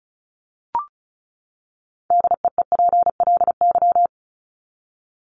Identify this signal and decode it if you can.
Morse code, heard as a keyed tone, about 34 wpm, 700 Hz. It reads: DEEPLY
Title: Top 1751-2000 Words 35wpm